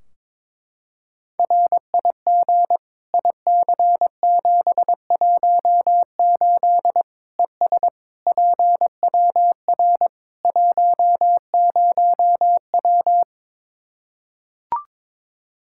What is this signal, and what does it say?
Morse code, heard as a keyed tone, about 22 wpm, 700 Hz. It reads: RIG IC718 ES PWR 10W
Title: Practice QSO - Exchange 2 - Info - 22wpm